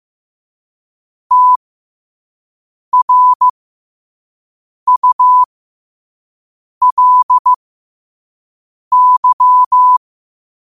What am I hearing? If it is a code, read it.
Morse code, heard as a keyed tone, about 15 wpm, spaced as for 5 wpm, 1000 Hz. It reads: TRULY
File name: qotdfiles / 5wpm